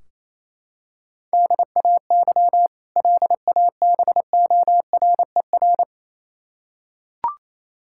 Morse code, heard as a keyed tone, about 28 wpm, 700 Hz. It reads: DAY LABORER
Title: K1USN SST QSO Practice 28wpm